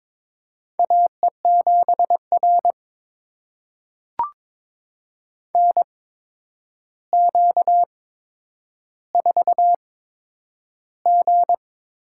Morse code, heard as a keyed tone, about 22 wpm, spaced as for 9 wpm, 700 Hz. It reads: AE7R NQ4G
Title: USA Call Signs - 4 Characters - 8x Character Spacing - 22wpm